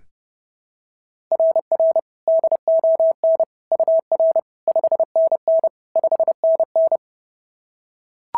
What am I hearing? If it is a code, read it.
Morse code, heard as a keyed tone, about 32 wpm, 690 Hz. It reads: RR DON UR 5NN 5NN
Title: SOTA QSO Practice 30wpm